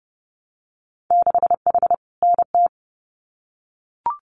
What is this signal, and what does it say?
Morse code, heard as a keyed tone, about 30 wpm, 700 Hz. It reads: 6H NT
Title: Winter Field Day Exchanges Encore 30wpm